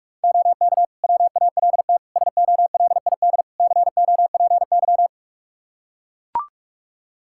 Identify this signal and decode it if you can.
Morse code, heard as a keyed tone, about 42 wpm, 700 Hz. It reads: OK WALT SOLID COPY
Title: Practice QSO - Exchange 3 - More Info - 45wpm